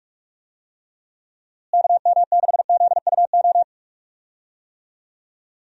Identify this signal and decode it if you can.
Morse code, heard as a keyed tone, about 46 wpm, 700 Hz. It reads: KM6ZUO
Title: USA Call Signs - 6 Characters - Repeated 2x - 45wpm